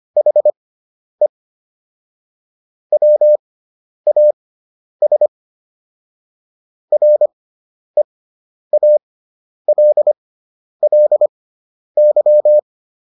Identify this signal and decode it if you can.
Morse code, heard as a keyed tone, about 25 wpm, spaced as for 10 wpm, 600 Hz. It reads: HE WAS REALLY